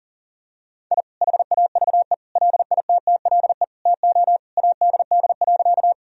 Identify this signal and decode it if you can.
Morse code, heard as a keyed tone, about 40 wpm, 700 Hz. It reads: I HAVE LITTLE TO ADD.
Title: Sentences from Top 1000 Words - Review Part 1 - 40wpm